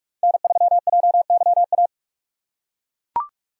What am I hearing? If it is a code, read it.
Morse code, heard as a keyed tone, about 42 wpm, 700 Hz. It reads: N3JYA
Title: USA Call Signs - 2x Character Spacing - 45wpm